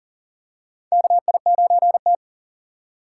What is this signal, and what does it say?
Morse code, heard as a keyed tone, about 40 wpm, 700 Hz. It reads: KI9T
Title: USA Call Signs - 4 Characters - 4x Character Spacing - 40wpm